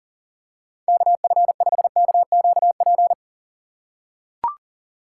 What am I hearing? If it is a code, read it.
Morse code, heard as a keyed tone, about 40 wpm, 700 Hz. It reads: KF5KQP
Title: USA Call Signs - 6 Characters - 4x Character Spacing - 40wpm